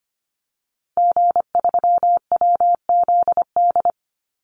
Morse code, heard as a keyed tone, about 25 wpm, 700 Hz. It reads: G3WZD
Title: Call Signs Repeated 2x - 25wpm